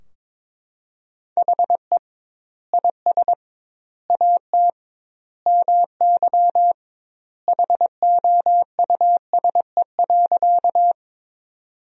Morse code, heard as a keyed tone, about 22 wpm, 700 Hz. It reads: HE IS AT MY HOUSE.
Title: 2x Word-Spacing - Sentences from Top 200 Words - 22wpm